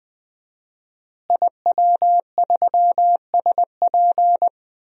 Morse code, heard as a keyed tone, about 20 wpm, 700 Hz. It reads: IW3SP
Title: DXCC-100 Call Signs - 5 Characters - Repeated 2x - 20wpm